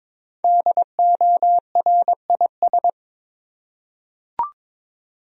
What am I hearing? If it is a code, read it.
Morse code, heard as a keyed tone, about 22 wpm, 700 Hz. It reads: DORIS
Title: USA Common Names - Max 5 Letters - 22wpm